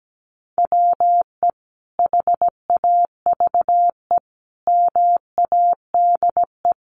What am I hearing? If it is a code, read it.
Morse code, heard as a keyed tone, about 17 wpm, 700 Hz. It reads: WE HAVE MADE
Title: Sentences from Top 1400 Words 17wpm